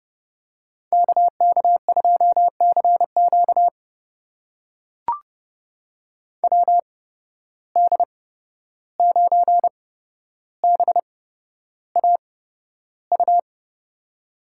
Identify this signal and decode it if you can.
Morse code, heard as a keyed tone, about 30 wpm, spaced as for 12 wpm, 700 Hz. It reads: KK2CQ WD9BAU
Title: USA Call Signs - 8x Character Spacing - 30wpm